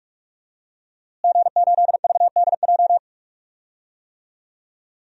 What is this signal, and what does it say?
Morse code, heard as a keyed tone, about 41 wpm, 700 Hz. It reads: G8VDJ
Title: DXCC-100 Call Signs - 5 Characters - Repeated 2x - 45wpm